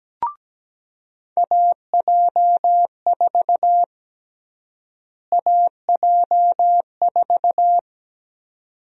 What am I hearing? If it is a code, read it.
Morse code, heard as a keyed tone, about 17 wpm, 700 Hz. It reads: AJ4 AJ4
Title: USA Call Sign Prefixes - Repeated 2x - 17wpm